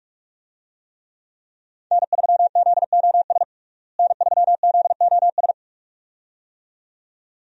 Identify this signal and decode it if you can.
Morse code, heard as a keyed tone, about 44 wpm, 700 Hz. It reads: N3ZOS N3ZOS
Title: USA Call Signs - 5 Characters - Repeated 2x - 45wpm